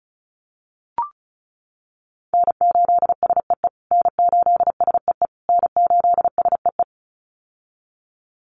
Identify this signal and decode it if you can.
Morse code, heard as a keyed tone, about 34 wpm, 700 Hz. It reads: N8SEE N8SEE N8SEE
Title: USA Call Signs Repeated 3x 35wpm